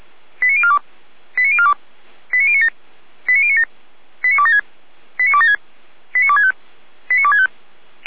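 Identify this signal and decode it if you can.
Five-tone selective call (ZVEI), heard as quick runs of five tones, tones of about 70 ms. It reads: 89042 89042 89097 89097 89276 89276 89265 89265
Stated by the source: ZVEI five-tone call 89277 received by SDR, archived by feuerwehr-orlamuende.org fire brigade server.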